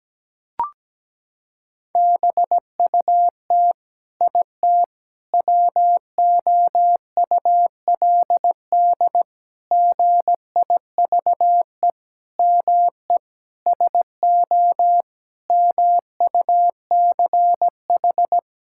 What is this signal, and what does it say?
Morse code, heard as a keyed tone, about 17 wpm, 700 Hz. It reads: BUT IT WOULD GIVE ME SO MUCH
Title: Sentences from Top 1600 Words 17wpm